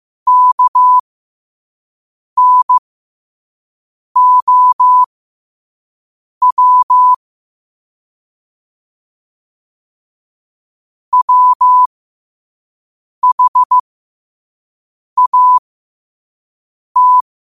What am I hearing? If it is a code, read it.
Morse code, heard as a keyed tone, about 15 wpm, spaced as for 5 wpm, 1000 Hz. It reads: KNOW WHAT
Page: qotd-at-5-wpm